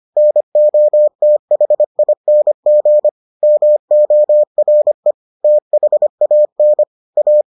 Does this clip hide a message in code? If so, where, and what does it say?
Yes: Morse code from 0.2 to 7.4 s, heard as a keyed tone, about 25 wpm, 600 Hz. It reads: NOTHING MORE THAN A